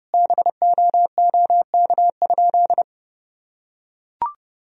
Morse code, heard as a keyed tone, about 30 wpm, 700 Hz. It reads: BOOK?
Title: Sentences from Top 1000 Words - Review Part 3 - 30wpm